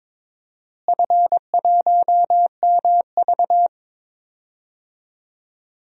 Morse code, heard as a keyed tone, about 22 wpm, 700 Hz. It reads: F1MV
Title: DXCC-100 Call Signs - 4 Characters - Repeated 3x - 22wpm